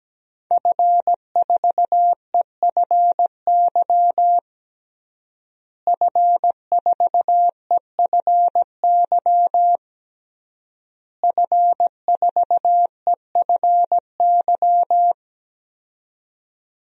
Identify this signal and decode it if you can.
Morse code, heard as a keyed tone, about 17 wpm, 700 Hz. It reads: F4EFY F4EFY F4EFY
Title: DXCC-100 Call Signs - 5 Characters - Repeated 3x - 17wpm